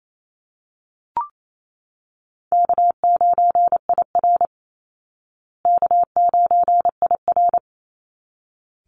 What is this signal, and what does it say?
Morse code, heard as a keyed tone, about 28 wpm, 700 Hz. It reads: K9IR K9IR
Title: Easy Call Signs Repeated 2x - 28wpm